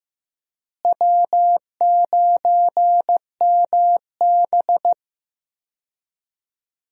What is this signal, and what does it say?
Morse code, heard as a keyed tone, about 15 wpm, 700 Hz. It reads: W9MB
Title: USA Call Signs - 4 Characters - Repeated 3x - 15wpm